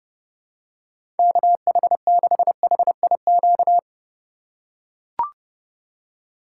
Morse code, heard as a keyed tone, about 30 wpm, 700 Hz. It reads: KH6HIQ
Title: USA Call Signs - 6 Characters - 2x Character Spacing - 30wpm